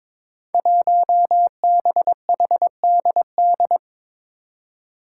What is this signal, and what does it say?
Morse code, heard as a keyed tone, about 22 wpm, 700 Hz. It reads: 1BHDD
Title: Five Challenging Letter-Number Combinations 22wpm